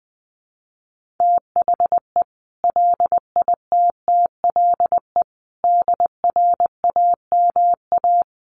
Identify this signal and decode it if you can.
Morse code, heard as a keyed tone, about 20 wpm, 700 Hz. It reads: THE LITTLE DRAMA
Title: Sentences from Top 1500 Words 20wpm